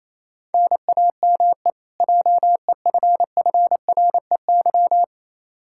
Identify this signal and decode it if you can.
Morse code, heard as a keyed tone, about 28 wpm, 700 Hz. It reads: NAME JEFFREY
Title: Practice QSO - Exchange 1 - Beginning - 28wpm